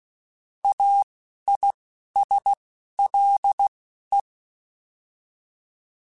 Morse code, heard as a keyed tone, about 16 wpm, spaced as for 12 wpm, 800 Hz. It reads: AISLE